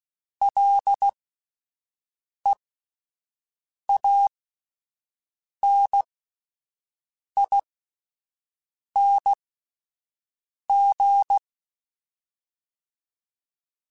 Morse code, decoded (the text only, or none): LEANING